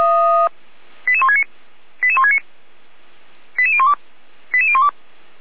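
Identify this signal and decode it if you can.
Five-tone selective call (ZVEI), heard as quick runs of five tones, tones of about 70 ms. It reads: 88169 88169 88012 88012
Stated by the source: Five-tone ZVEI selective call received by SDR, fire brigade (Feuerwehr) Orlamünde.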